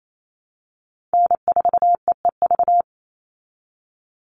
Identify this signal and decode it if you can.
Morse code, heard as a keyed tone, about 28 wpm, 700 Hz. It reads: N4EEV
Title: Call Signs Repeated 3x - 28wpm